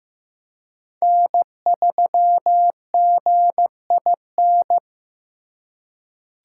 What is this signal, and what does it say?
Morse code, heard as a keyed tone, about 15 wpm, 700 Hz. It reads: N3GIN
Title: USA Call Signs - 5 Characters - 3x Character Spacing - 15wpm